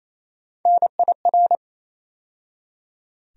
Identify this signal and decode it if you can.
Morse code, heard as a keyed tone, about 28 wpm, 700 Hz. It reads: NIR
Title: USA Call Sign Postfixes - 3 Letters 28wpm